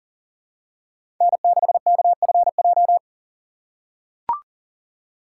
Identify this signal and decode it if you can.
Morse code, heard as a keyed tone, about 40 wpm, 700 Hz. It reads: N6KFJ
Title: USA Call Signs - 5 Characters - 2x Character Spacing - 40wpm